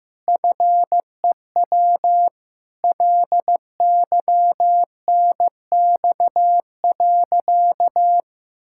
Morse code, heard as a keyed tone, about 16 wpm, 700 Hz. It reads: FEW LYNX.
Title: Pangram Challenge v2 - 15wpm